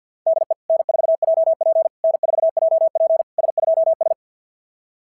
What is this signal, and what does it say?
Morse code, heard as a keyed tone, about 46 wpm, 660 Hz. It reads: DE N4JP N4JP S2S
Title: SOTA QSO Practice 50wpm